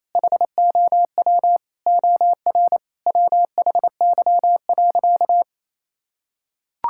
Morse code, heard as a keyed tone, about 28 wpm, 700 Hz. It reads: HOW OR WHY.
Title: Sentences from Top 1000 Words - Review Part 6 - 28wpm